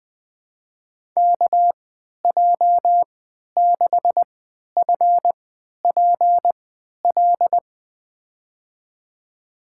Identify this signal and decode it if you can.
Morse code, heard as a keyed tone, about 20 wpm, spaced as for 11 wpm, 700 Hz. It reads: KJ6FPL
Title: USA Call Signs - 3x Character Spacing - 20wpm